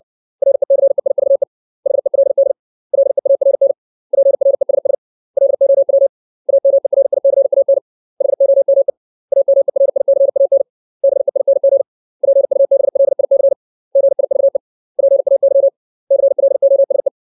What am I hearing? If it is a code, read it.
Morse code, heard as a keyed tone, about 60 wpm, 550 Hz. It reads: RECEIVE HELD LEARN PRESS LOW AMERICAN HOME AMERICAN BEING PUBLIC GIVE PAY CLOSE
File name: 632 words / 632 Words 60WPM